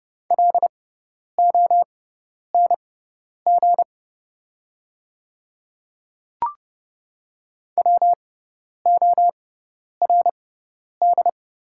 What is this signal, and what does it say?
Morse code, heard as a keyed tone, about 30 wpm, spaced as for 10 wpm, 700 Hz. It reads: LONG WORD
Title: 6x Character Spacing - Top 300 Words - 30wpm